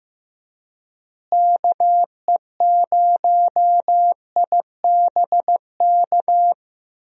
Morse code, heard as a keyed tone, about 15 wpm, 700 Hz. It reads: KE0IBK
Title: USA Call Signs - 6 Characters - Repeated 2x - 15wpm